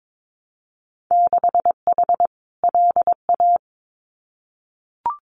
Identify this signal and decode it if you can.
Morse code, heard as a keyed tone, about 22 wpm, 700 Hz. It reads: 6H LA
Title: Winter Field Day Exchanges Encore 22wpm